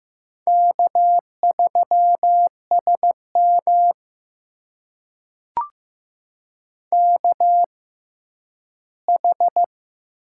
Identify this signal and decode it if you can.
Morse code, heard as a keyed tone, about 15 wpm, spaced as for 10 wpm, 700 Hz. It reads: K3SM KH
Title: USA Call Signs - 4 Characters - 6x Character Spacing - 15wpm